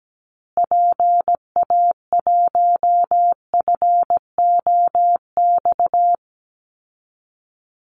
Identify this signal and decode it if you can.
Morse code, heard as a keyed tone, about 17 wpm, 700 Hz. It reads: PA1FOX
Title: Call Signs Repeated 2x - 17wpm